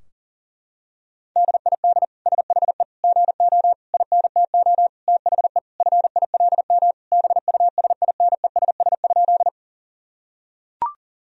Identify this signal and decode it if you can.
Morse code, heard as a keyed tone, about 37 wpm, 700 Hz. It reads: DID SHE GO INTO THE FILM BUSINESS?
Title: Sentences from Top 1000 Words - Review Part 3 - 40wpm